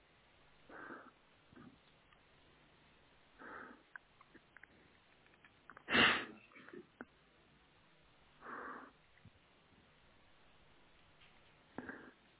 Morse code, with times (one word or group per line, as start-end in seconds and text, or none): none